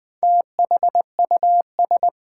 Morse code, heard as a keyed tone, about 20 wpm, 700 Hz. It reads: THUS
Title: Four Letter Words - 20wpm